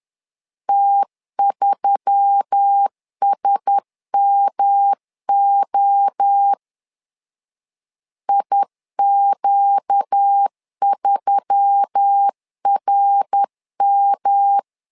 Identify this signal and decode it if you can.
Morse code, heard as a keyed tone, about 10 wpm, 800 Hz. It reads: T3SMO IQ3RM
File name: LEZIONI / 4 Q1S3 10wpm 2427